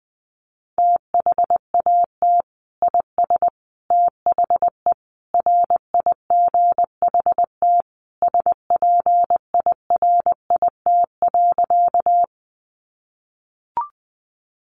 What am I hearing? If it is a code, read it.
Morse code, heard as a keyed tone, about 20 wpm, 700 Hz. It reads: THAT IS THE RIGHT SPIRIT.
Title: Sentences from Top 1400 Words 20wpm